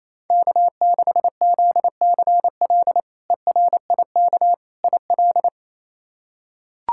Morse code, heard as a keyed tone, about 28 wpm, 700 Hz. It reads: K6ZCL ERIK IL
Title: K1USN SST QSO Practice - 28wpm